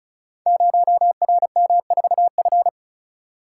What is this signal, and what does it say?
Morse code, heard as a keyed tone, about 36 wpm, 700 Hz. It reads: 0RM4F
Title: Code Groups - 5 Letter:Numbers - 35wpm